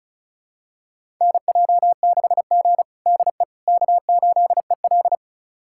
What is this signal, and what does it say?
Morse code, heard as a keyed tone, about 36 wpm, 700 Hz. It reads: NJ6G DE K8EL
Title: USA Callsign DE Callsign - 4 Characters - 35wpm